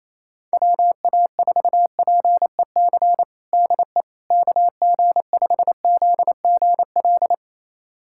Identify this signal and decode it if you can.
Morse code, heard as a keyed tone, about 27 wpm, 700 Hz. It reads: WA4PEC DE KG5ZGL
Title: USA Callsign DE Callsign - 6 Characters - 28wpm